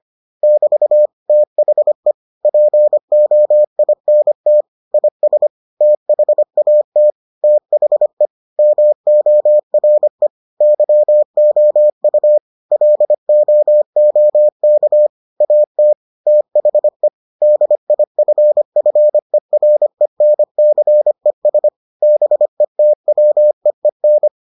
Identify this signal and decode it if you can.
Morse code, heard as a keyed tone, about 25 wpm, 600 Hz. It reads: <BT> THE POINT IS THAT THE MORE YOU LOOK AT THE DIFFERENCES BETWEEN